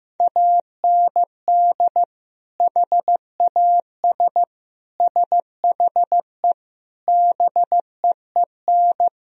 Morse code, heard as a keyed tone, about 15 wpm, 700 Hz. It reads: AND HAS SHE BEEN
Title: Sentences from Top 1000 Words - Review Part 1 - 15wpm